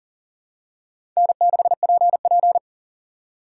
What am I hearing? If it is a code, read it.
Morse code, heard as a keyed tone, about 40 wpm, 700 Hz. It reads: N6PP
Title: USA Call Signs - 4 Characters - 2x Character Spacing - 40wpm